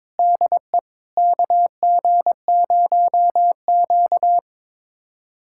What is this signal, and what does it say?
Morse code, heard as a keyed tone, about 22 wpm, 700 Hz. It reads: DE KG0Q
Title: USA Callsign DE Callsign - 4 Characters - 22wpm